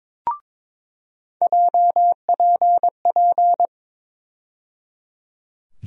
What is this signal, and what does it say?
Morse code, heard as a keyed tone, about 21 wpm, 700 Hz. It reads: JPP
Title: USA Call Sign Postfixes - 3 Letters 22wpm